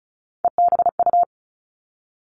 Morse code, heard as a keyed tone, about 36 wpm, 700 Hz. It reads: EBU
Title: Three Challenging Letter-Number Combinations 35wpm